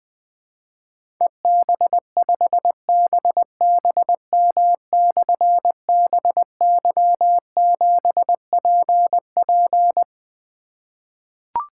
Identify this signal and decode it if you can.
Morse code, heard as a keyed tone, about 21 wpm, 700 Hz. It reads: EB5BBM/BY7PP
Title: Difficult Call Signs 20wpm